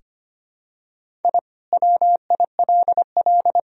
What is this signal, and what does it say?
Morse code, heard as a keyed tone, about 25 wpm, 700 Hz. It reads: I WILL
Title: Sentences from Top 1000 Words - Review Part 2 - 25wpm